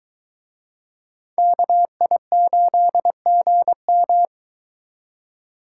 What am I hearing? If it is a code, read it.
Morse code, heard as a keyed tone, about 23 wpm, 700 Hz. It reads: KI8GM